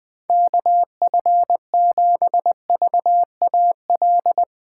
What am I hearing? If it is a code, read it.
Morse code, heard as a keyed tone, about 20 wpm, 700 Hz. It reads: KF7VAL
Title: USA Call Signs - 6 Characters - 2x Character Spacing - 20wpm